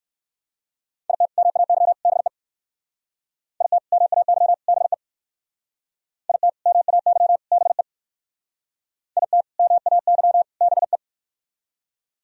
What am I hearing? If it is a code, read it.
Morse code, heard as a keyed tone, about 58 wpm, 700 Hz. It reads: IT MAY BE IT MAY BE IT MAY BE IT MAY BE